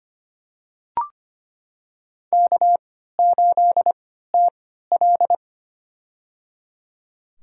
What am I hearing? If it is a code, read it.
Morse code, heard as a keyed tone, about 25 wpm, spaced as for 14 wpm, 700 Hz. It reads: K8TL
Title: USA Call Signs - 3x Character Spacing - 25wpm